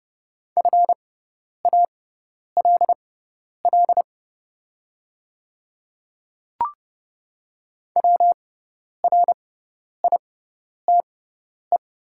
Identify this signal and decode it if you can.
Morse code, heard as a keyed tone, about 30 wpm, spaced as for 10 wpm, 700 Hz. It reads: FALL WRITE
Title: 6x Character Spacing - Top 500 Words - 30wpm